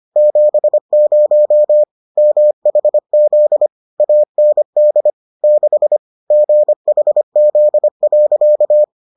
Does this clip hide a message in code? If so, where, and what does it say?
Yes: Morse code from 0.2 to 8.8 s, heard as a keyed tone, about 25 wpm, 600 Hz. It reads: 70 MHZ AND 6 GHZ.